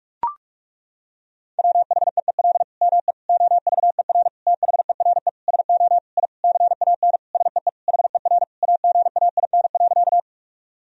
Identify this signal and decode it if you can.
Morse code, heard as a keyed tone, about 41 wpm, 700 Hz. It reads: WHEEL ME OVER THERE SO I CAN SEE HER AGAIN.
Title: Sentences from Top 1000 Words - Review Part 3 - 45wpm